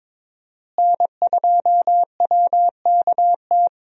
Morse code, heard as a keyed tone, about 22 wpm, 700 Hz. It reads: N2WKT
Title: USA Call Signs - 5 Characters - Repeated 3x - 22wpm